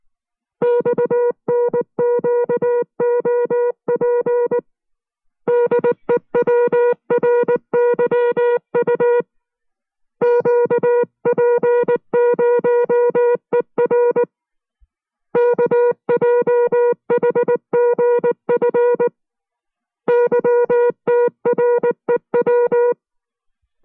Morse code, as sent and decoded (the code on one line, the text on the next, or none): -..- -. --.- --- .--. / -.. .-- .-. -.-- ..- / --.- .--. ----- . .-. / -.- .--- .... --. ..-. / -.-- - .-. .--
XNQOP DEWRYU QP0ER KJHGF YTREW